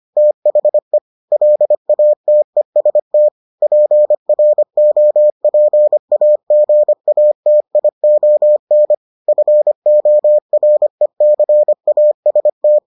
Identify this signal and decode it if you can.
Morse code, heard as a keyed tone, about 25 wpm, 600 Hz. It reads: THE LATEST PROPAGATION FORECAST